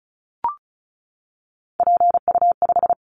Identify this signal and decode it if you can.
Morse code, heard as a keyed tone, about 34 wpm, 700 Hz. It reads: PU5
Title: Callsign Prefixes for Top 50 DXCC - 35wpm